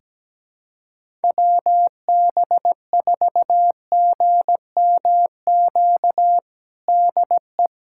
Morse code, heard as a keyed tone, about 17 wpm, 700 Hz. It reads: WB4GMQ DE
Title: USA Callsign DE Callsign - 6 Characters - 17wpm